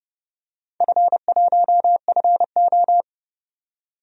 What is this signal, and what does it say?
Morse code, heard as a keyed tone, about 30 wpm, 700 Hz. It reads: F1FO
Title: DXCC-100 Call Signs - 4 Characters - 30wpm